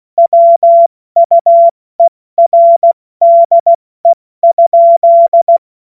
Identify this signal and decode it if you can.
Morse code, heard as a keyed tone, about 16 wpm, 680 Hz. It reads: WUERDE?
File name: Morsekurs / Witze / Witz 15. Juni 2021, 70 F80